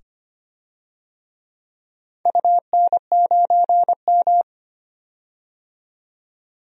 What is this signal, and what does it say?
Morse code, heard as a keyed tone, about 25 wpm, 700 Hz. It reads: UN9M
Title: DXCC-100 Call Signs - 4 Characters - 25wpm